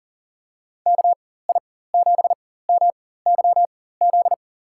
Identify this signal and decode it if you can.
Morse code, heard as a keyed tone, about 40 wpm, spaced as for 19 wpm, 700 Hz. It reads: KI7MYZ
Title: USA Call Signs - 6 Characters - 4x Character Spacing - 40wpm